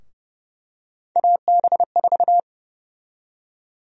Morse code, heard as a keyed tone, about 30 wpm, 700 Hz. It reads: AB4
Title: USA Call Sign Prefixes - Repeated 2x - 30wpm